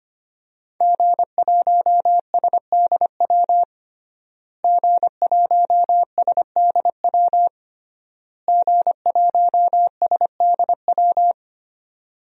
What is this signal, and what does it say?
Morse code, heard as a keyed tone, about 25 wpm, 700 Hz. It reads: G1SDW G1SDW G1SDW
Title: DXCC-100 Call Signs - 5 Characters - Repeated 3x - 25wpm